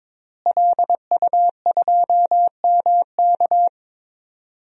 Morse code, heard as a keyed tone, about 22 wpm, 700 Hz. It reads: LU2MK
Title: DXCC-100 Call Signs - 5 Characters - Repeated 3x - 22wpm